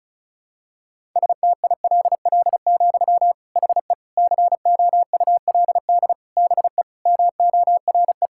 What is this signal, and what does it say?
Morse code, heard as a keyed tone, about 36 wpm, 700 Hz. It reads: STILL, HE COULD BE MORE
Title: Sentences from Top 1000 Words - Review Part 5 - 35wpm